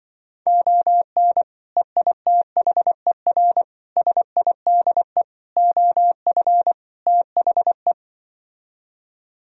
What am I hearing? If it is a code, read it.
Morse code, heard as a keyed tone, about 24 wpm, 700 Hz. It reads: ON EITHER SIDE OF THE
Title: Speed-Racing - Sets of 5 Words 20wpm